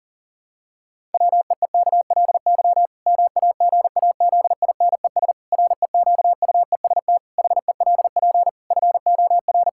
Morse code, heard as a keyed tone, about 40 wpm, 700 Hz. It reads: WEEKLY MAGAZINES REQUEST HELP FOR
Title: Pangram Challenge v2 - 40wpm